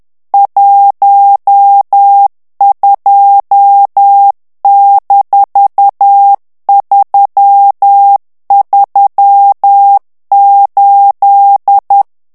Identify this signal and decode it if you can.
Morse code, heard as a keyed tone, about 10 wpm, 800 Hz. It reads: 12-338
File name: ANGLAIS texte clair 3 vitesse 10 mots minute